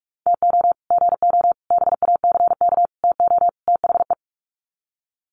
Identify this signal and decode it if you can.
Morse code, heard as a keyed tone, about 44 wpm, 700 Hz. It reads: TO GO BACK TO THE
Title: Sets of 5 Words 45wpm